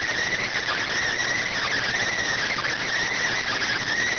Emergency Alert System (SAME) tones, heard as harsh buzzing bursts. It is heard from 0.2 to 3.8 s.